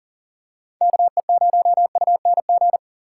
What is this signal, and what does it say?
Morse code, heard as a keyed tone, about 39 wpm, 700 Hz. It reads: KE0UNG
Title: USA Call Signs - 6 Characters - 6x Character Spacing - 40wpm